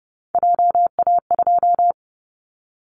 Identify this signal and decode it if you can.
Morse code, heard as a keyed tone, about 30 wpm, 700 Hz. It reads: JA2
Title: Callsign Prefixes for Top 50 DXCC - 30wpm